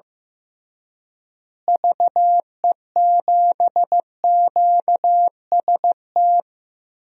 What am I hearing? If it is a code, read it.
Morse code, heard as a keyed tone, about 15 wpm, 700 Hz. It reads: VE7QST
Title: DXCC-100 Call Signs - 6 Characters - Repeated 3x - 15wpm